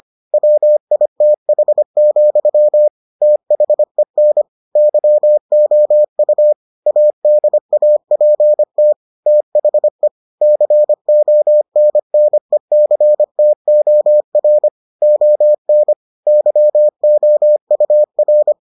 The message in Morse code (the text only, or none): WITH, THEN YOU ADAPT THE CONNECTOR ON YOUR